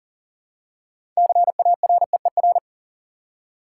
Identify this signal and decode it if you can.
Morse code, heard as a keyed tone, about 40 wpm, 700 Hz. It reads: CAREER
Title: Six Letter Words - 40wpm